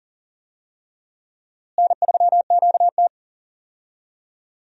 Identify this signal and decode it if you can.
Morse code, heard as a keyed tone, about 40 wpm, 700 Hz. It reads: N3QT
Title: USA Call Signs - 4 Characters - Repeated 3x - 40wpm